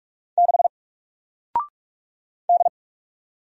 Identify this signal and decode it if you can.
Morse code, heard as a keyed tone, about 46 wpm, 700 Hz. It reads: 6 D
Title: Single Letter-Number Alt 45wpm